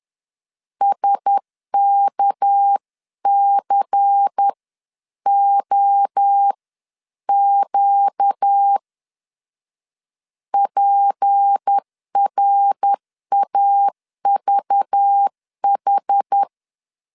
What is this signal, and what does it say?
Morse code, heard as a keyed tone, about 10 wpm, spaced as for 9 wpm, 800 Hz. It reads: SKCOQ PRAVH
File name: LEZIONI / 7 GF9Z / 7 GF 10wpm 2514